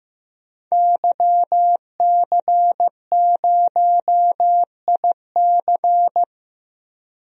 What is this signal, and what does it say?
Morse code, heard as a keyed tone, about 15 wpm, 700 Hz. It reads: YC0IC
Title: DXCC-100 Call Signs - 5 Characters - 15wpm